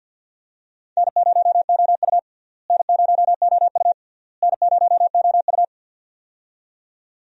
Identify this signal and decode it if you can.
Morse code, heard as a keyed tone, about 47 wpm, 700 Hz. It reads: N0OU N0OU N0OU
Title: USA Call Signs - 4 Characters - Repeated 3x - 50wpm